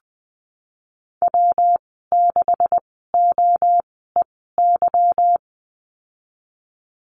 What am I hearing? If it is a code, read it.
Morse code, heard as a keyed tone, about 19 wpm, spaced as for 14 wpm, 700 Hz. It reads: W6OEY
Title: 167 Lesson 27g - Number 7 - Call Signs 20wpm